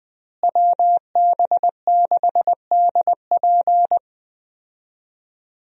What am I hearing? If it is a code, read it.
Morse code, heard as a keyed tone, about 20 wpm, 700 Hz. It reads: WB6DP
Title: USA Call Signs - 5 Characters - Repeated 2x - 20wpm